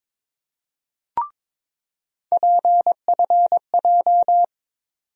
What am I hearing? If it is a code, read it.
Morse code, heard as a keyed tone, about 21 wpm, 700 Hz. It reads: PFJ
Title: USA Call Sign Postfixes - 3 Letters 22wpm